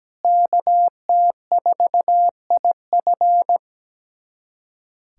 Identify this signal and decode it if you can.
Morse code, heard as a keyed tone, about 17 wpm, 700 Hz. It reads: KT4IF
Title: USA Callsign DE Callsign - 5 Characters - 17wpm